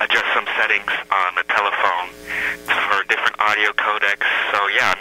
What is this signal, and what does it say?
Emergency Alert System (SAME) tones, heard as harsh buzzing bursts, with speech. It is heard from 0.5 to 4.4 s.
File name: mp3files / off the hook overtime -128